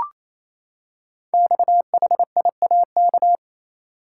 Morse code, heard as a keyed tone, about 28 wpm, 700 Hz. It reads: XHIAK
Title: Code Groups - 5 Letter:Numbers - 28wpm